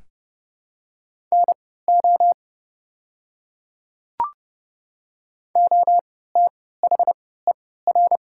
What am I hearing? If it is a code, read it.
Morse code, heard as a keyed tone, about 30 wpm, spaced as for 17 wpm, 700 Hz. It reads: NO OTHER
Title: 3x Character Spacing - Top 100 Words - 30wpm